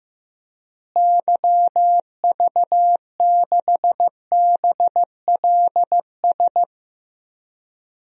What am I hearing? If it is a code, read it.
Morse code, heard as a keyed tone, about 16 wpm, 700 Hz. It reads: YV6BLS
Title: DXCC-100 Call Signs - 6 Characters - Repeated 2x - 15wpm